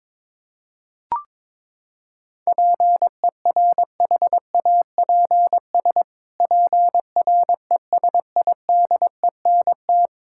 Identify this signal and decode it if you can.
Morse code, heard as a keyed tone, about 22 wpm, 700 Hz. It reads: PERHAPS PRESIDENT
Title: Pangram Challenge v2 - 22wpm